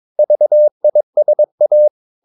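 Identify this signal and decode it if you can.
Morse code, heard as a keyed tone, about 22 wpm, 600 Hz. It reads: VISA
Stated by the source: Morse practice of short Danish words at 22 wpm, from CW med Gnister.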